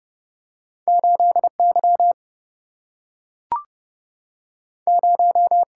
Morse code, heard as a keyed tone, about 30 wpm, 700 Hz. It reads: 8Y 0
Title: Single-Letter Single-Number 30wpm